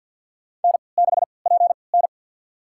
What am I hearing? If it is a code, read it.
Morse code, heard as a keyed tone, about 52 wpm, spaced as for 28 wpm, 700 Hz. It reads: N6PN
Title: USA Call Signs - 4 Characters - 3x Character Spacing - 50wpm